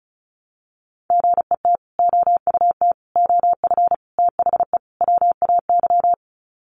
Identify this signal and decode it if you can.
Morse code, heard as a keyed tone, about 34 wpm, 700 Hz. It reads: GET OUT OF THE WAY
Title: Sets of 5 Words Encore 35wpm